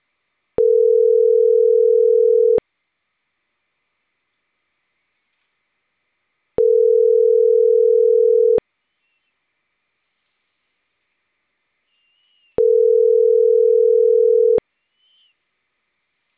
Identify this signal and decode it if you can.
Ringback tone (North America), 440 + 480 Hz.